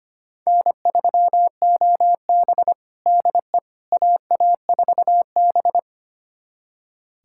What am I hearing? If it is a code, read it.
Morse code, heard as a keyed tone, about 25 wpm, 700 Hz. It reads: N3OB DE AA4B
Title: USA Callsign DE Callsign - 4 Characters - 25wpm